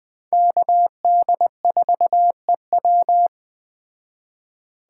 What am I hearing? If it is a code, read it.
Morse code, heard as a keyed tone, about 21 wpm, 700 Hz. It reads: KD4EW